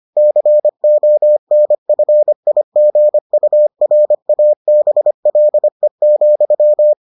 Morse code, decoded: CONFIGURABLE,

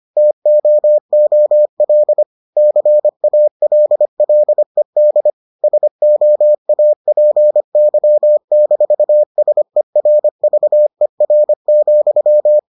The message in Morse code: TOOL CALLED SOAPY-SERVER,

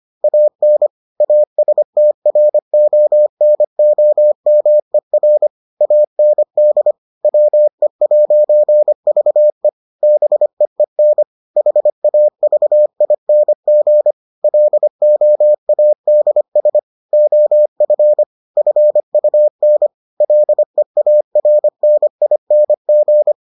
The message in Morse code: AN ASTRONOMER AND WE'VE BEEN HAVING LOADS OF FUN LEARNING

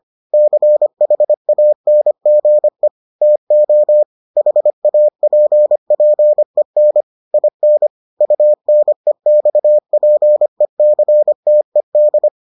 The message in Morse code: CHANGE TO HAPPEN IN UNEXPECTED

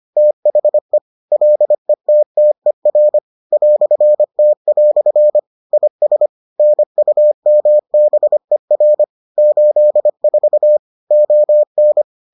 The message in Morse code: THE LETTER "T" IS NUMBER 84 ON

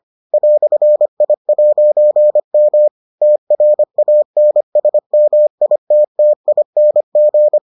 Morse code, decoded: "I'M TRANSMITTING

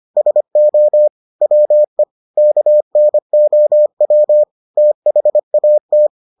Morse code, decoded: SO WE KNOW THAT

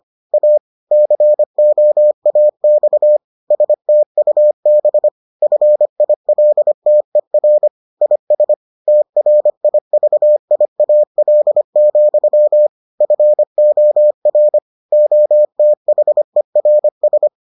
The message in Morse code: A COAX STUB FILTER IS TRIVIAL, FOR OTHERS

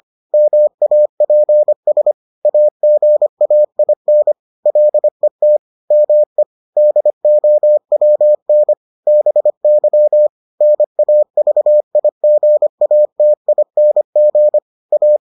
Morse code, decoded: MAPS AGAIN LET ME DOWN BY NAVIGATING A

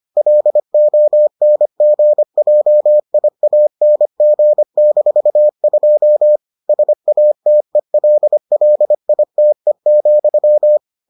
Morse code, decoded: LONGJIANG-2 SATELLITE,

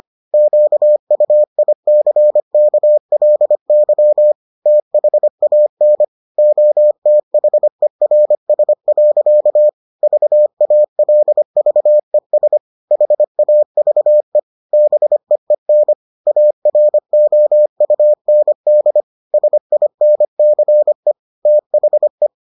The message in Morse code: QUICKLY THAN OTHERS. VALVES HAVE BEEN AROUND SINCE THE

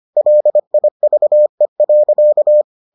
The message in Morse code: LIVE.

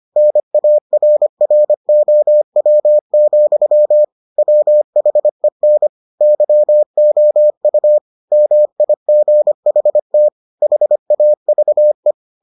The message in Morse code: NARROW, WHEN YOU MIGHT HAVE

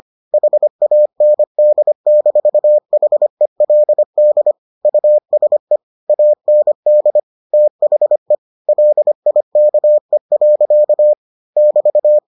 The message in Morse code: HAND-HELD USE AND THE LIKE. <BT>

